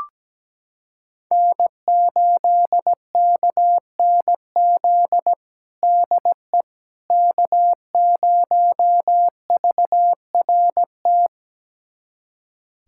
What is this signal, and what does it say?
Morse code, heard as a keyed tone, about 17 wpm, 700 Hz. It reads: N8KNZ DE K0VRT
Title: USA Callsign DE Callsign - 5 Characters - 17wpm